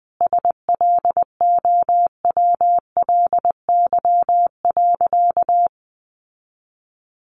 Morse code, heard as a keyed tone, about 20 wpm, 700 Hz. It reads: SLOWLY.
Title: Sentences from Top 1100 Words 20wpm